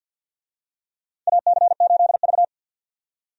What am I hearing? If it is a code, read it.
Morse code, heard as a keyed tone, about 52 wpm, 700 Hz. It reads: AC8V
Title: USA Call Signs - 4 Characters - 50wpm